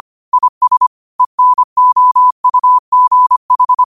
Morse code, heard as a keyed tone, about 25 wpm, 1000 Hz. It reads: IS ENOUGH
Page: New quotes every day in morse code at 25 Words per minute.